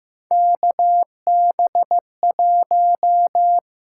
Morse code, heard as a keyed tone, about 15 wpm, 700 Hz. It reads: KB1
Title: USA Call Signs - 6 Characters - 8x Character Spacing - 15wpm